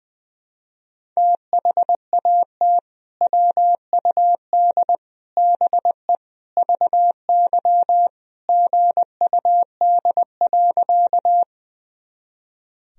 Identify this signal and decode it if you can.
Morse code, heard as a keyed tone, about 21 wpm, 700 Hz. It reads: THAT WUD BE VY GUD.
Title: Sentences from Top 200 Words and Abbreviations 20wpm